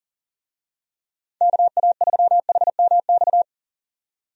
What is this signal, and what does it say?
Morse code, heard as a keyed tone, about 40 wpm, 700 Hz. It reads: KA3HMX
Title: USA Call Signs - 6 Characters - 40wpm